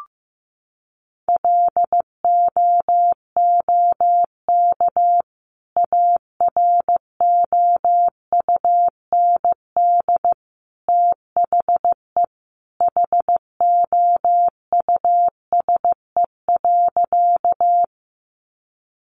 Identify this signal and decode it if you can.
Morse code, heard as a keyed tone, about 15 wpm, 700 Hz. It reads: LOOK AROUND THE HOUSE.
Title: Sentences from Top 600 Words 15wpm